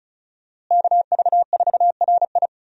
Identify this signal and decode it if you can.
Morse code, heard as a keyed tone, about 36 wpm, 700 Hz. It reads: KV4RI
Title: USA Call Signs - 5 Characters - Repeated 3x - 35wpm